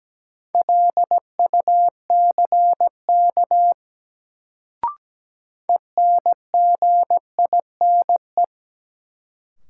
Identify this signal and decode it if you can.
Morse code, heard as a keyed tone, about 17 wpm, 700 Hz. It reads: LUCK ENGINE
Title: Sentences from Top 1500 Words - New Word Drill - 17wpm